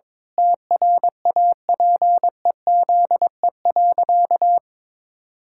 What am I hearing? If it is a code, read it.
Morse code, heard as a keyed tone, about 22 wpm, 700 Hz. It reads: TRAPEZE.
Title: Pangram Challenge v2 - 22wpm